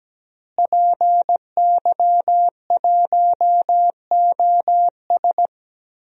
Morse code, heard as a keyed tone, about 17 wpm, 700 Hz. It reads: PY1OS
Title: DXCC-100 Call Signs - 5 Characters - Repeated 3x - 17wpm